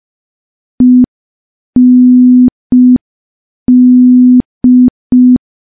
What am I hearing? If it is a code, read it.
Morse code, heard as a keyed tone, about 5 wpm, 250 Hz. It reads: END
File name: morse / morse-training-05WPM